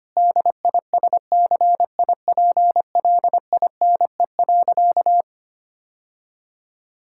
Morse code, heard as a keyed tone, about 25 wpm, 700 Hz. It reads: DISCIPLINE.
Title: Sentences from Top 1000 Words - Review Part 6 - 25wpm